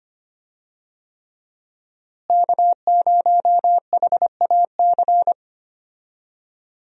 Morse code, heard as a keyed tone, about 25 wpm, 700 Hz. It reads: K0HAC
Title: USA Call Signs - 5 Characters - Repeated 3x - 25wpm